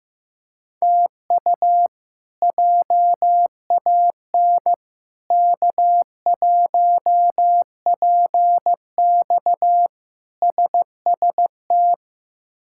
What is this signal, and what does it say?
Morse code, heard as a keyed tone, about 15 wpm, 700 Hz. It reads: TU JAN K1PX SST
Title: K1USN SST QSO Practice - 15wpm